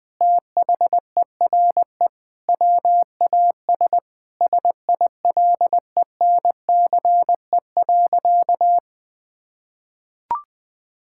Morse code, heard as a keyed tone, about 20 wpm, 700 Hz. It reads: THERE WAS SILENCE.
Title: Sentences with 5-Word Phrases - 20wpm